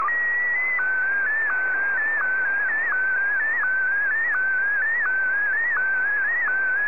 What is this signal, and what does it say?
SSTV (slow-scan television), PD 90.